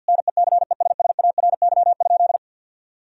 Morse code, heard as a keyed tone, about 50 wpm, 700 Hz. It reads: NECESSARY?
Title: Sentences from Top 1000 Words - Review Part 3 - 50wpm